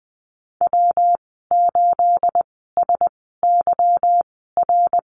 Morse code, heard as a keyed tone, about 20 wpm, spaced as for 15 wpm, 700 Hz. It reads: W8SYR
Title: 238 Lesson 38g - Letter q - Call Signs 20wpm